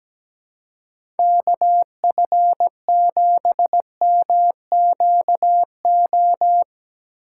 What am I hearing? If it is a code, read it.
Morse code, heard as a keyed tone, about 17 wpm, 700 Hz. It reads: KF7MQO
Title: USA Call Signs - 6 Characters - 3x Character Spacing - 17wpm